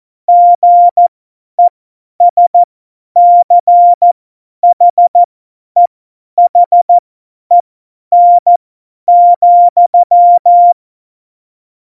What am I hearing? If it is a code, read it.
Morse code, heard as a keyed tone, about 14 wpm, spaced as for 10 wpm, 700 Hz. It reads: GESCHEHEN,
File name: Morsekurs / Witze / Witz 01. August 17, Tempo 50 F70